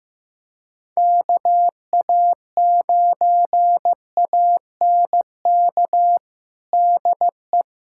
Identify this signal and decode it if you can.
Morse code, heard as a keyed tone, about 15 wpm, 700 Hz. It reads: KA9ANK DE
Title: USA Callsign DE Callsign - 6 Characters - 15wpm